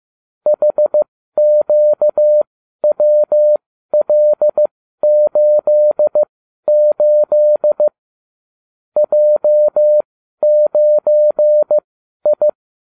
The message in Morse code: HQWL88 J9I

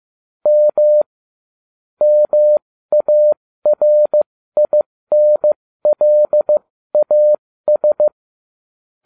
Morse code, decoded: M MARINLAS

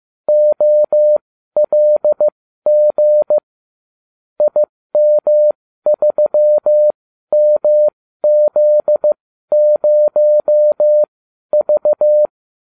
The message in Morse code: OLG IM3MZ0V

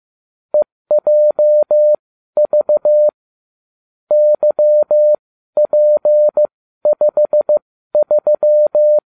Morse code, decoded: EJV YP53

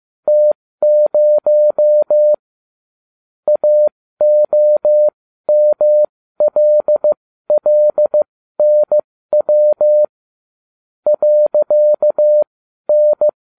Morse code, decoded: T0 AOMLLNW .N

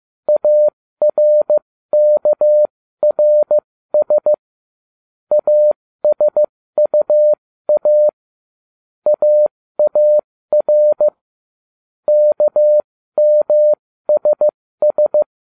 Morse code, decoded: ARKRS ASUA AAR KMSS

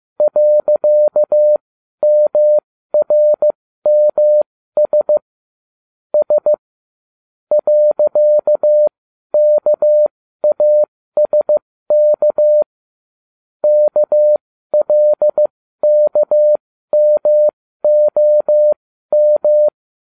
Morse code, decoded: .MRMS S .KASK KLKMOM

